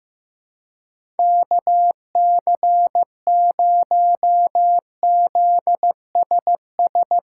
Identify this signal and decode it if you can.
Morse code, heard as a keyed tone, about 15 wpm, 700 Hz. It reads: KC0ZSS
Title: USA Callsign DE Callsign 15wpm